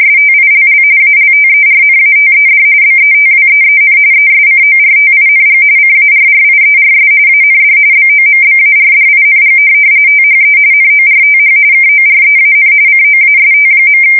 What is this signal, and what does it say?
RTTY, about 45.45 baud. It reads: STILL ALTERNATED BETWEEN ESCORTING CARRIERS AND BOMBARDING THE NORTH KOREAN COAST.
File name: fsk170 / USS-New-Jersey